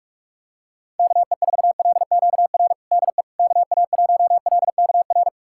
Morse code, heard as a keyed tone, about 46 wpm, 700 Hz. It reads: KE4LQR DE KA1LKR
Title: USA Callsign DE Callsign - 6 Characters - 45wpm